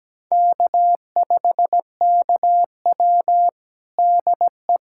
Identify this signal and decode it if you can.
Morse code, heard as a keyed tone, about 17 wpm, 700 Hz. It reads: K5KW DE
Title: USA Callsign DE Callsign - 4 Characters - 17wpm